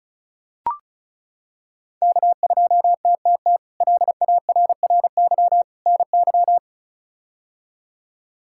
Morse code, heard as a keyed tone, about 36 wpm, 700 Hz. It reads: K2TTT LARRY NY
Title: K1USN SST QSO Practice - 35wpm